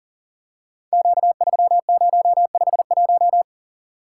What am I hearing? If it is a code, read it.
Morse code, heard as a keyed tone, about 42 wpm, 690 Hz. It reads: Q3051
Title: Code Groups - 5 Letter:Numbers - 40wpm